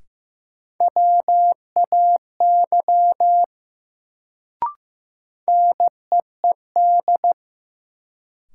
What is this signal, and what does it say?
Morse code, heard as a keyed tone, about 15 wpm, 700 Hz. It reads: WAY NEED
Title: Sets of 2 Words - New Word Drill - 15wpm